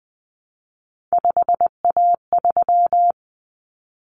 Morse code, heard as a keyed tone, about 20 wpm, 700 Hz. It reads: 5A3
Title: Common Cut Numbers - 3-Digit 20wpm